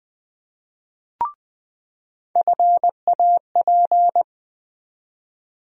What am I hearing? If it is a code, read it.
Morse code, heard as a keyed tone, about 21 wpm, 700 Hz. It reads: FAP
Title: USA Call Sign Postfixes - 3 Letters 20wpm